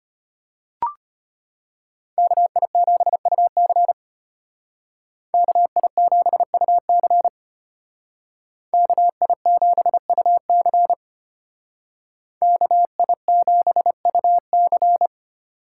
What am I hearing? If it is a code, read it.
Morse code, heard as a keyed tone, about 32 wpm, 700 Hz. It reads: KI7UC KI7UC KI7UC KI7UC